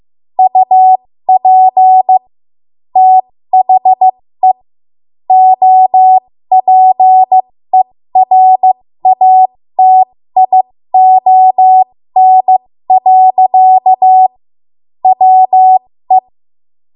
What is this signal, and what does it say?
Morse code, heard as a keyed tone, about 15 wpm, spaced as for 13 wpm, 750 Hz. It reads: UP THE OPERATION. WE
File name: Archive / 13 WPM / 13WPM